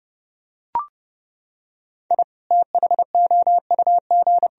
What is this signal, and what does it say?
Morse code, heard as a keyed tone, about 32 wpm, 700 Hz. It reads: I THOUG